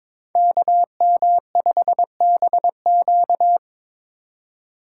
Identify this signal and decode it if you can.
Morse code, heard as a keyed tone, about 22 wpm, 700 Hz. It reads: KM5BQ
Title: USA Call Signs - 5 Characters - 3x Character Spacing - 22wpm